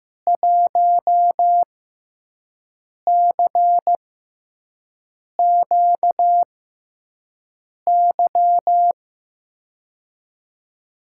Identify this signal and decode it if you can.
Morse code, heard as a keyed tone, about 15 wpm, spaced as for 5 wpm, 700 Hz. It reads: 1CQY
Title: USA Call Signs - 6x Character Spacing - 15wpm